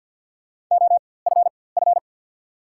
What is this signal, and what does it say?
Morse code, heard as a keyed tone, about 54 wpm, spaced as for 24 wpm, 690 Hz. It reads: OFF
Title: 4x Character Spacing - Top 300 Words - 50wpm